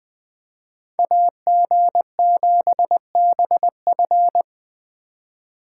Morse code, heard as a keyed tone, about 23 wpm, 700 Hz. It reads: AG7BF
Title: Speed Racing - USA 2x2 Call Signs 20wpm